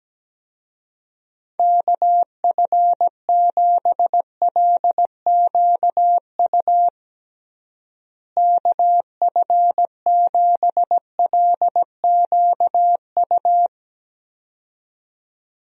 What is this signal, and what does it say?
Morse code, heard as a keyed tone, about 17 wpm, 700 Hz. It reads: KF7LQU KF7LQU
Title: USA Call Signs - 6 Characters - Repeated 2x - 17wpm